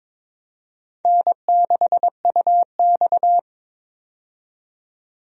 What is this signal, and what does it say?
Morse code, heard as a keyed tone, about 22 wpm, 700 Hz. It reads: N6UX
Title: USA Call Signs - 4 Characters - 22wpm